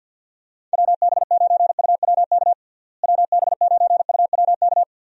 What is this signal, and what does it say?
Morse code, heard as a keyed tone, about 49 wpm, 700 Hz. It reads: WB9UWK WB9UWK
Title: USA Call Signs - 6 Characters - Repeated 2x - 50wpm